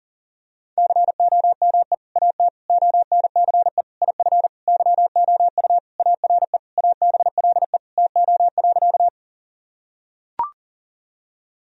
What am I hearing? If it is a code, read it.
Morse code, heard as a keyed tone, about 40 wpm, 700 Hz. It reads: COME AT ONCE IF YOU ARE ABLE TO.
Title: Sentences from Top 1000 Words - Review Part 2 - 40wpm